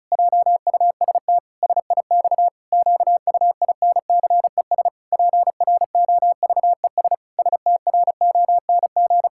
Morse code, heard as a keyed tone, about 34 wpm, 700 Hz. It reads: JUST SIX QUINCES PROVES STRONG
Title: Pangram Challenge v2 - 35wpm